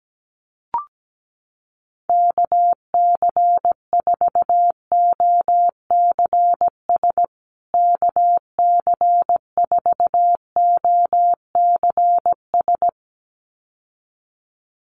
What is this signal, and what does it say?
Morse code, heard as a keyed tone, about 17 wpm, 700 Hz. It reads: KC4OCS KC4OCS
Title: USA Call Signs Repeated 2x 17wpm